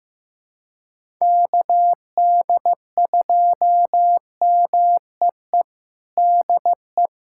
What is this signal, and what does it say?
Morse code, heard as a keyed tone, about 16 wpm, 700 Hz. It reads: KD2MEE DE
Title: USA Callsign DE Callsign - 6 Characters - 15wpm